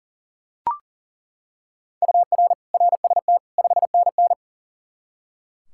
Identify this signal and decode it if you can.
Morse code, heard as a keyed tone, about 40 wpm, 700 Hz. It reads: UR RST 5NN
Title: Practice QSO - Exchange 1 - Beginning - 40wpm